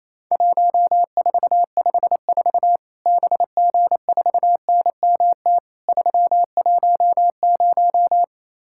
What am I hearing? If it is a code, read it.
Morse code, heard as a keyed tone, about 28 wpm, 700 Hz. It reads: 1454 BG4NMT 310
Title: Worked All Europe DX Contest - QTC Exchange - 28wpm